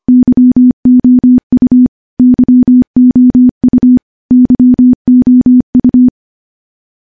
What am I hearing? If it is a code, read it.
Morse code, heard as a keyed tone, about 25 wpm, 250 Hz. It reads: YOU YOU YOU
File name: morse-training-25WPM